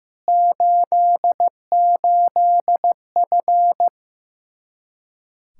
Morse code, heard as a keyed tone, about 15 wpm, 700 Hz. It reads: 88F
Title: Practice QSO - Exchange 2 - Info - 15wpm